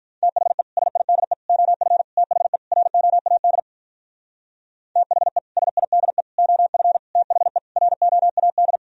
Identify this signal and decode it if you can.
Morse code, heard as a keyed tone, about 45 wpm, 700 Hz. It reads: THE SIDE OF THE ROAD THE SIDE OF THE ROAD